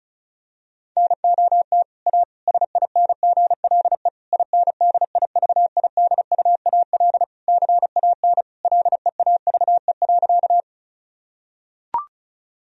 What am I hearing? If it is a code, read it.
Morse code, heard as a keyed tone, about 36 wpm, 700 Hz. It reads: NOT A SINGLE INDIVIDUAL CAN LEAVE.
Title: Sentences from Top 1000 Words - Review Part 4 - 35wpm